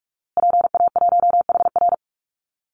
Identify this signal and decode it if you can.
Morse code, heard as a keyed tone, about 44 wpm, 700 Hz. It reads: PA1HR
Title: Call Signs Repeated 2x - 45wpm